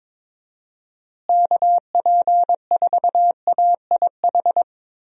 Morse code, heard as a keyed tone, about 22 wpm, 700 Hz. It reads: KP4AIH
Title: DXCC-100 Call Signs - 6 Characters - Repeated 2x - 22wpm